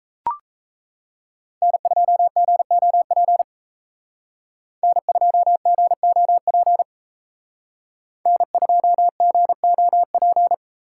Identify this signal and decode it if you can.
Morse code, heard as a keyed tone, about 35 wpm, 700 Hz. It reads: N2GOP N2GOP N2GOP
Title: Speed Racing - USA Call Signs 28wpm